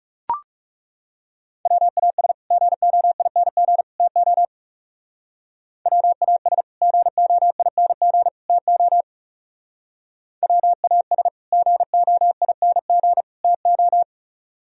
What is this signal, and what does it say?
Morse code, heard as a keyed tone, about 39 wpm, 700 Hz. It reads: WAS GOING TO WAS GOING TO WAS GOING TO